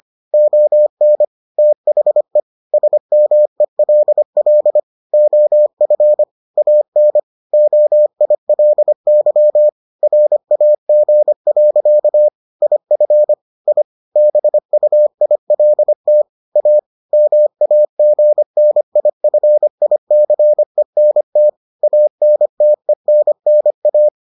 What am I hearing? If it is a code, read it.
Morse code, heard as a keyed tone, about 25 wpm, 600 Hz. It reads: ON THE SMELL OF AN OILY RAG. IF I BUILT A MAGNIFICENT ANTENNA